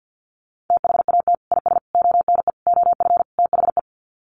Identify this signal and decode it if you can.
Morse code, heard as a keyed tone, about 50 wpm, 700 Hz. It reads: THAT IS ONE OF THE
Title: Sets of 5 Words Encore 50wpm